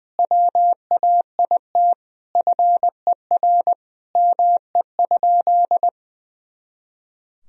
Morse code, heard as a keyed tone, about 20 wpm, 700 Hz. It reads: WAIT FER ME?
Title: Sentences from Top 500 Words and Abbreviations 20wpm